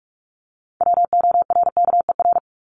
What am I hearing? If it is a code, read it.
Morse code, heard as a keyed tone, about 44 wpm, 700 Hz. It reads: WORKER
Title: Top 1751-2000 Words 45wpm